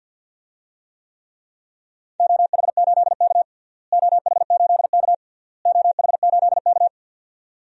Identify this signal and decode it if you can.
Morse code, heard as a keyed tone, about 47 wpm, 700 Hz. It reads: OH8K OH8K OH8K
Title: DXCC-100 Call Signs - 4 Characters - Repeated 3x - 50wpm